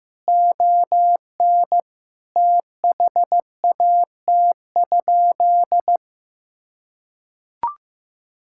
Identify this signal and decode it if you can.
Morse code, heard as a keyed tone, about 15 wpm, 700 Hz. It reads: ON THAT?
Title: Sentences from Top 1000 Words - Review Part 5 - 15wpm